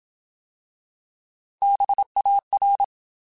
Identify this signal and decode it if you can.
Morse code, heard as a keyed tone, about 27 wpm, 800 Hz. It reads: BAR
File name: Alphabetical / Alphabetical mp3s / Bs @25